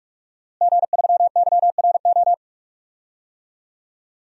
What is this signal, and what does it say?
Morse code, heard as a keyed tone, about 42 wpm, 700 Hz. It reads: G3YRO
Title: DXCC-100 Call Signs - 5 Characters - Repeated 3x - 45wpm